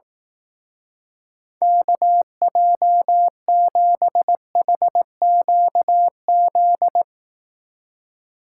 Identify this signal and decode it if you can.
Morse code, heard as a keyed tone, about 18 wpm, 700 Hz. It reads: KJ7HQZ
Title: Speed Racing - USA 2x3 Call Signs 15wpm